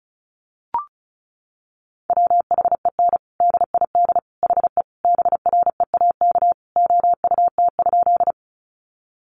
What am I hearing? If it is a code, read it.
Morse code, heard as a keyed tone, about 34 wpm, 700 Hz. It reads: WHEN DID HE BREAK OUT?
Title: Sentences from Top 900 Words 35wpm